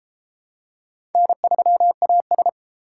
Morse code, heard as a keyed tone, about 29 wpm, 700 Hz. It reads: N3AS